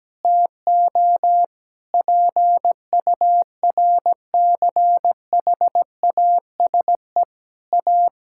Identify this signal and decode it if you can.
Morse code, heard as a keyed tone, about 17 wpm, 700 Hz. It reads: TO PURCHASE A
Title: Sentences from Top 1000 Words - Review Part 5 - 17wpm